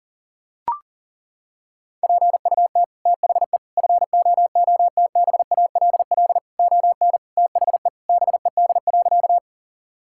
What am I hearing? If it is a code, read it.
Morse code, heard as a keyed tone, about 40 wpm, 700 Hz. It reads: PUT THE FOOTBALL ON THE BED.
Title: Sentences from Top 1600 Words 40wpm